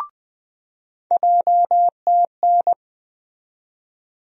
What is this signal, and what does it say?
Morse code, heard as a keyed tone, about 20 wpm, 700 Hz. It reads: JTN
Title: USA Call Sign Postfixes - 3 Letters 20wpm